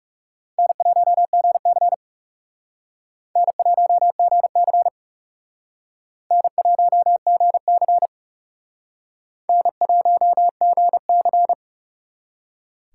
Morse code, heard as a keyed tone, about 37 wpm, 700 Hz. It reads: N1GC N1GC N1GC N1GC
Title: Speed Racing - USA 1x2 Call Signs 30wpm